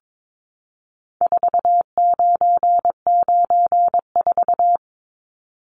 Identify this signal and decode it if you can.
Morse code, heard as a keyed tone, about 22 wpm, 700 Hz. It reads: 4994
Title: Numbers - 4-Digit 22wpm